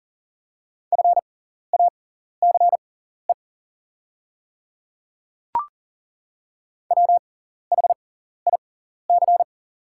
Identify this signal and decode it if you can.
Morse code, heard as a keyed tone, about 40 wpm, spaced as for 14 wpm, 700 Hz. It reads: FACE WHIC
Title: 6x Character Spacing - Top 300 Words - 40wpm